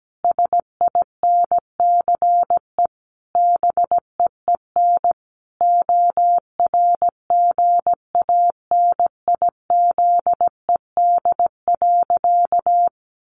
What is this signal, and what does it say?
Morse code, heard as a keyed tone, about 17 wpm, 700 Hz. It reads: SINCE BEEN ORGANIZED.
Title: Sentences from Top 1400 Words 17wpm